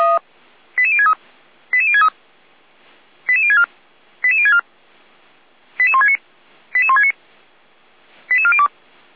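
Five-tone selective call (ZVEI), heard as quick runs of five tones, tones of about 70 ms. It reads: 88063 88063 88064 88064 88169 88169 88402
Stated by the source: ZVEI five-tone call received on SDR.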